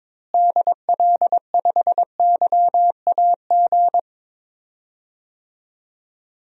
Morse code, heard as a keyed tone, about 21 wpm, 700 Hz. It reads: DL5YAG
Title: DXCC-100 Call Signs - 6 Characters - Repeated 3x - 22wpm